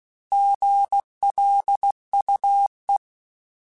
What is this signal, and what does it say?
Morse code, heard as a keyed tone, about 16 wpm, 800 Hz. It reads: GLUE
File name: Alphabetical / Alphabetical mp3s / Gs @15